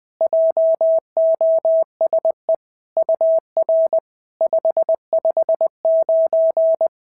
Morse code, heard as a keyed tone, about 20 wpm, 650 Hz. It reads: JOSE UR 559
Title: SOTA QSO Practice 20wpm